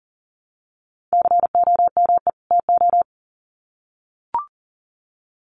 Morse code, heard as a keyed tone, about 40 wpm, 700 Hz. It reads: COME TO
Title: Sets of 2 Words 40wpm